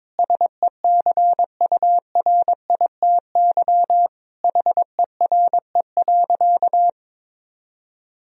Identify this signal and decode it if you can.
Morse code, heard as a keyed tone, about 21 wpm, 700 Hz. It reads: SECURITY HERE.
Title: Sentences from Top 1000 Words - Review Part 4 - 22wpm